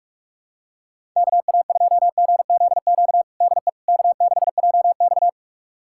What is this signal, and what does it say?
Morse code, heard as a keyed tone, about 44 wpm, 690 Hz. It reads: KA2GZQ DE K6JX
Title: USA Callsign DE Callsign 45wpm